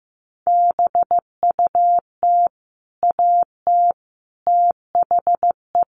Morse code, heard as a keyed tone, about 15 wpm, 700 Hz. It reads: BUT AT THE
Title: Sets of 5 Words Encore2 15wpm